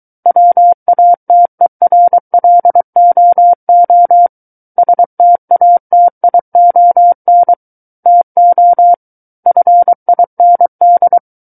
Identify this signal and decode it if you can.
Morse code, heard as a keyed tone, about 23 wpm, 700 Hz. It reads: WATERLOO STATION TO FIND
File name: Book One / 14-Chapter 14 - 23 WPM